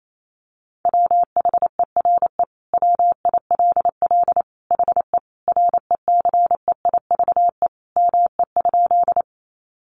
Morse code, heard as a keyed tone, about 28 wpm, 700 Hz. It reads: WHERE WILL HE RECEIVE ME?
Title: Sentences from Top 1000 Words 28wpm